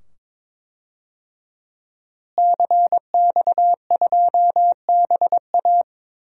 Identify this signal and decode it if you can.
Morse code, heard as a keyed tone, about 22 wpm, 700 Hz. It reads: CX2BA
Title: DXCC-100 Call Signs - 5 Characters - Repeated 2x - 22wpm